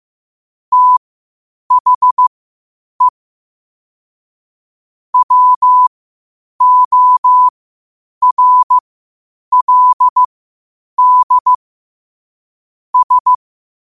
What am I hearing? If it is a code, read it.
Morse code, heard as a keyed tone, about 15 wpm, spaced as for 8 wpm, 1000 Hz. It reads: THE WORLD S
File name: qotdfiles / 8wpm